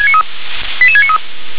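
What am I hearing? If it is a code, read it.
Five-tone selective call (ZVEI), heard as quick runs of five tones, tones of about 70 ms. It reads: 88693
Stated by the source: ZVEI five-tone selective call received on an SDR.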